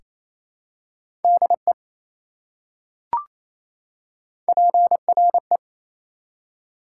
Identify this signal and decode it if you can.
Morse code, heard as a keyed tone, about 28 wpm, 700 Hz. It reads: DE PRE
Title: Prefixes and Suffixes 28wpm